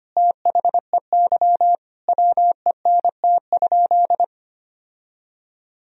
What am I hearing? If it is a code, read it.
Morse code, heard as a keyed tone, about 25 wpm, 700 Hz. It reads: THEY WENT?
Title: Sentences from Top 1000 Words - Review Part 3 - 25wpm